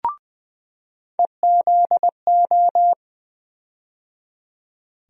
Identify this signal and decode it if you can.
Morse code, heard as a keyed tone, about 20 wpm, 700 Hz. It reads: EZO
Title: USA Call Sign Postfixes - 3 Letters 20wpm